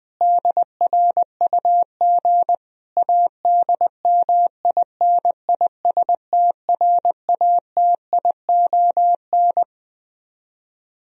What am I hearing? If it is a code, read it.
Morse code, heard as a keyed tone, about 21 wpm, 700 Hz. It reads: DRUG ADMINISTRATION